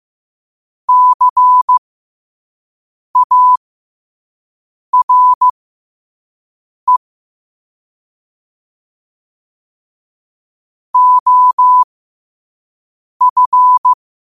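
Morse code, heard as a keyed tone, about 15 wpm, spaced as for 5 wpm, 1000 Hz. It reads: CARE OF
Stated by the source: Morse code practice at 5 wpm.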